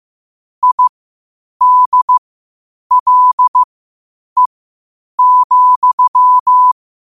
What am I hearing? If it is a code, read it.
Morse code, heard as a keyed tone, about 15 wpm, spaced as for 8 wpm, 1000 Hz. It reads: IDLE,
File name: qotdfiles / 8wpm